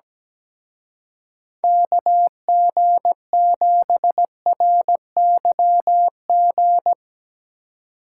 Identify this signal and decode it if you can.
Morse code, heard as a keyed tone, about 19 wpm, 700 Hz. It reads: KG7RYG
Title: Speed Racing - USA Call Signs 17wpm